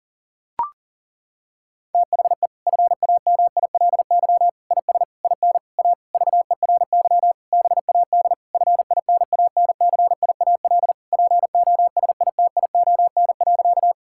Morse code, heard as a keyed tone, about 40 wpm, 700 Hz. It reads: THE FAMILY IS IN A VERY BAD FINANCIAL POSITION.
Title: Sentences from Top 1000 Words - Review Part 5 - 40wpm